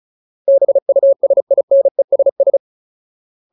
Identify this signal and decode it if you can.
Morse code, heard as a keyed tone, about 34 wpm, 550 Hz. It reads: BUSINESS
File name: Ultimate Word List / 35WPM Ultimate